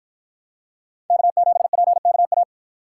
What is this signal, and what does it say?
Morse code, heard as a keyed tone, about 47 wpm, 700 Hz. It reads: K7PKA